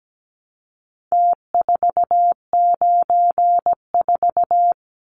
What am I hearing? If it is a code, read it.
Morse code, heard as a keyed tone, about 17 wpm, 700 Hz. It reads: T494
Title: Numbers - Contest Style - 4-Digit 17wpm